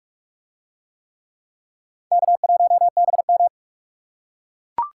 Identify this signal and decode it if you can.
Morse code, heard as a keyed tone, about 44 wpm, 700 Hz. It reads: K1BM